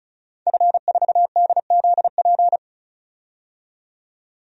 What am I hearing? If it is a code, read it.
Morse code, heard as a keyed tone, about 36 wpm, 700 Hz. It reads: F4DZP
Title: DXCC-100 Call Signs - 5 Characters - 35wpm